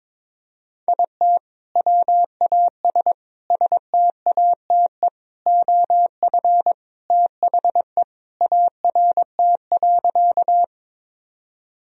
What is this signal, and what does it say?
Morse code, heard as a keyed tone, about 22 wpm, 700 Hz. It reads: IT WAS STATE OF THE ART.
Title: Sentences from Top 1000 Words - Review Part 3 - 22wpm